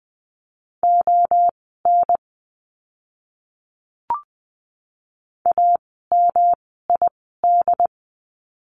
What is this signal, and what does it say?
Morse code, heard as a keyed tone, about 20 wpm, spaced as for 14 wpm, 700 Hz. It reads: ON AMID
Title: 108 Lesson 18f - Letter m - Words 20wpm